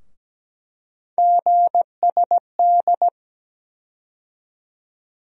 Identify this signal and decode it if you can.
Morse code, heard as a keyed tone, about 17 wpm, 700 Hz. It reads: GSD
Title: USA Call Sign Postfixes - 3 Letters 17wpm